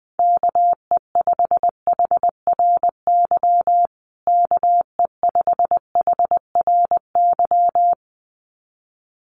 Morse code, heard as a keyed tone, about 20 wpm, 700 Hz. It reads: KE5HRY KE5HRY
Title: USA Call Signs Repeated 2x 20wpm